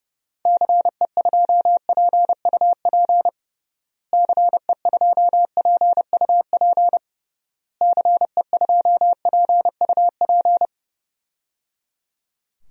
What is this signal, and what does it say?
Morse code, heard as a keyed tone, about 30 wpm, 700 Hz. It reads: CE2PUP CE2PUP CE2PUP
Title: DXCC-100 Call Signs - 6 Characters - Repeated 3x - 30wpm